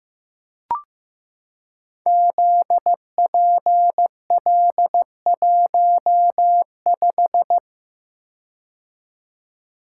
Morse code, heard as a keyed tone, about 15 wpm, 700 Hz. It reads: ZPL15
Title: Code Groups - 5 Letter:Numbers - 15wpm